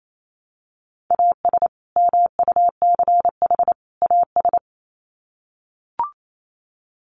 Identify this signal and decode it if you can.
Morse code, heard as a keyed tone, about 28 wpm, 700 Hz. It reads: AS MUCH AS
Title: Sets of 3 Words 28wpm